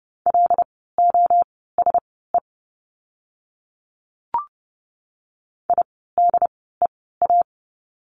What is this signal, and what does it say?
Morse code, heard as a keyed tone, about 30 wpm, spaced as for 17 wpm, 700 Hz. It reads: LOSE IDEA
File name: course / 30:10wpm / 108 Lesson 18f - Letter m - Words 30wpm